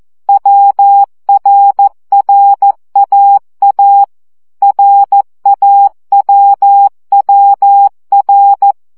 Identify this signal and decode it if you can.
Morse code, heard as a keyed tone, about 14 wpm, 800 Hz. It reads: WRRAA RAWWR
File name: CW gyakorlas a r w 12 baud